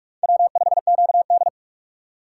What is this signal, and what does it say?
Morse code, heard as a keyed tone, about 44 wpm, 700 Hz. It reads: W5QD